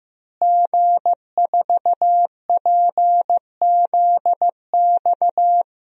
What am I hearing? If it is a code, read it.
Morse code, heard as a keyed tone, about 15 wpm, 700 Hz. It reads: G4PZX
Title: DXCC-100 Call Signs - 5 Characters - Repeated 3x - 15wpm